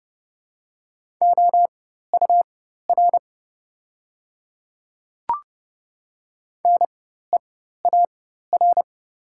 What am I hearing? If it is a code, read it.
Morse code, heard as a keyed tone, about 31 wpm, spaced as for 14 wpm, 700 Hz. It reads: OUR NEAR
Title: 4x Character Spacing - Top 200 Words - 30wpm